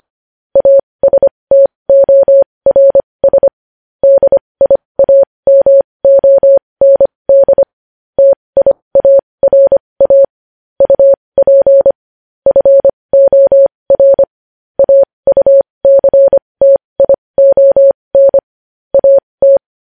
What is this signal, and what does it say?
Morse code, heard as a keyed tone, about 25 wpm, spaced as for 20 wpm, 550 Hz. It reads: ASTORS DIAMOND TIARA UP FOR AUCTION AT
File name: podcast / News headlines-1x-25x20